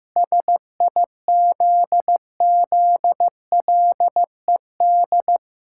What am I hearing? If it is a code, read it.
Morse code, heard as a keyed tone, about 15 wpm, 700 Hz. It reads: SIZZLED
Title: Pangram Challenge v2 - 15wpm